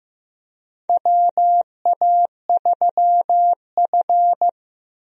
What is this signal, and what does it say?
Morse code, heard as a keyed tone, about 15 wpm, 700 Hz. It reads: WA3F
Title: USA Call Signs - 4 Characters - Repeated 3x - 15wpm